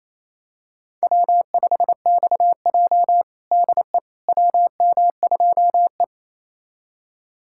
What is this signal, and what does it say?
Morse code, heard as a keyed tone, about 28 wpm, 700 Hz. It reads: W5XJ DE WM2E
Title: USA Callsign DE Callsign - 4 Characters - 28wpm